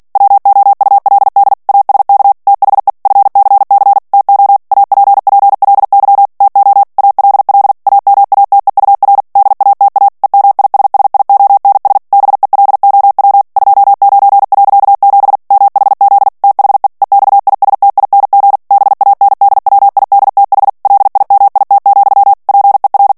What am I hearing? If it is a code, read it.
Morse code, heard as a keyed tone, about 48 wpm, 800 Hz. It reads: WOULD ASK THE FCC TO APPLY TO ALL AMATEUR DATA EMISSIONS BELOW 29.7 MHZ THE EXISTING BANDWIDTH LIMIT, PER
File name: ANGLAIS texte clair 3 vitesse 45 mots minute